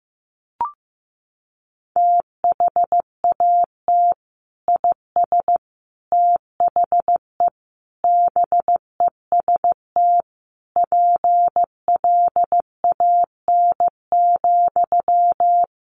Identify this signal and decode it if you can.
Morse code, heard as a keyed tone, about 15 wpm, 700 Hz. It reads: THAT IS THE BEST PLAN,
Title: Sentences from Top 1000 Words 15wpm